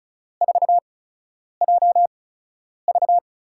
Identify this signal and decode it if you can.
Morse code, heard as a keyed tone, about 36 wpm, spaced as for 10 wpm, 700 Hz. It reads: 4JV
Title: USA Call Signs - 4 Characters - 8x Character Spacing - 35wpm